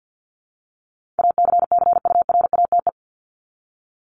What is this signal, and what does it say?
Morse code, heard as a keyed tone, about 50 wpm, 700 Hz. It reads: ACCURATE
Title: Top 1001-1250 Words 50wpm